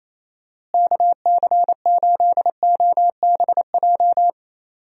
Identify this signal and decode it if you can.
Morse code, heard as a keyed tone, about 27 wpm, 700 Hz. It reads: KC8OBJ
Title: USA Call Signs - 6 Characters - Repeated 2x - 28wpm